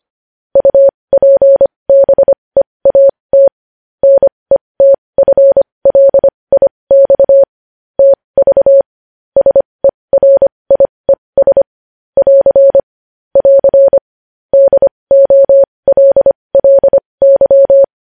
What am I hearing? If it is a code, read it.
Morse code, heard as a keyed tone, about 25 wpm, spaced as for 20 wpm, 550 Hz. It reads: UPBEAT NETFLIX TV SERIES <AR> <AR> DOLLY
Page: news-headlines-in-morse-code-at-20-wpm